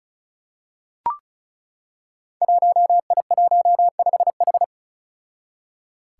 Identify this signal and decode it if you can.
Morse code, heard as a keyed tone, about 36 wpm, 700 Hz. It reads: 1I15H
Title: Five Challenging Letter-Number Combinations 35wpm